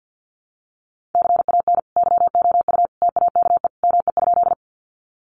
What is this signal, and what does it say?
Morse code, heard as a keyed tone, about 50 wpm, 700 Hz. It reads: CAN YOU TAKE ME?
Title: Sentences from Top 200 Words 50wpm